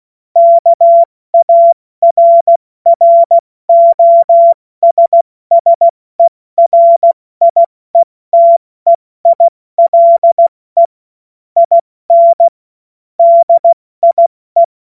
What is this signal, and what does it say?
Morse code, heard as a keyed tone, about 16 wpm, 680 Hz. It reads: KARROSSERIETEILE IN DIE
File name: Witz-14-Oktober-2025-70-F80